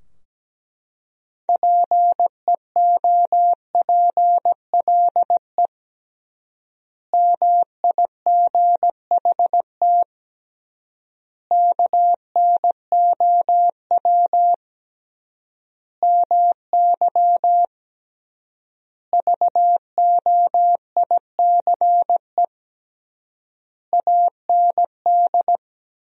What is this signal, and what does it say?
Morse code, heard as a keyed tone, about 17 wpm, 700 Hz. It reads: PEOPLE MIGHT KNOW MY VOICE AND
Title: 3x Word-Spacing - Sentences from Top 500 Words - 17wpm